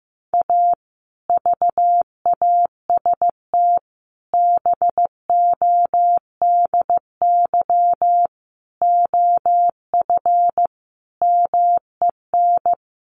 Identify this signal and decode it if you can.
Morse code, heard as a keyed tone, about 15 wpm, 700 Hz. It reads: A VAST BODY OF MEN
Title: Sentences from Top 1300 Words 15wpm